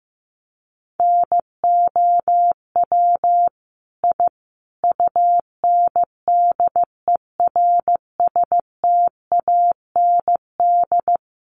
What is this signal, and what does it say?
Morse code, heard as a keyed tone, about 14 wpm, 700 Hz. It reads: NOW I UNDERSTAND
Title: Sentences from Top 1000 Words 15wpm